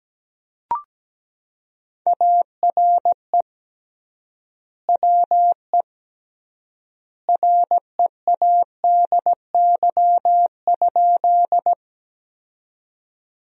Morse code, heard as a keyed tone, about 17 wpm, 700 Hz. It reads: ARE WE READY?
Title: 3x Word-Spacing - Sentences from Top 400 Words - 17wpm